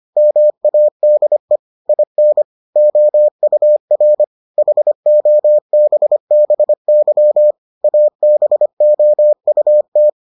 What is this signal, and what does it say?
Morse code, heard as a keyed tone, about 25 wpm, 600 Hz. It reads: MADE IN OUR HOBBY ABOUT